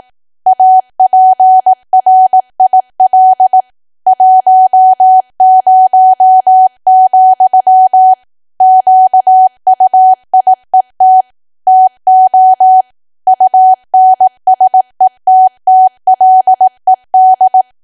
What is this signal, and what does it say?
Morse code, heard as a keyed tone, about 18 wpm, 750 Hz. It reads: APRIL 10, QUIET TO UNSETTLED
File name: cwa / W1AW-Level2 / 18WPM